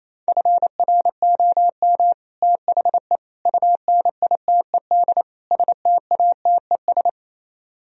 Morse code, heard as a keyed tone, about 33 wpm, 700 Hz. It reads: FROM THE UNITED STATES
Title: Speed-Racing - Sets of 4 Words 28wpm